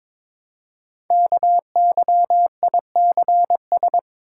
Morse code, heard as a keyed tone, about 22 wpm, 700 Hz. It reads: KYICS
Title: Code Groups - 5 Letters - 22wpm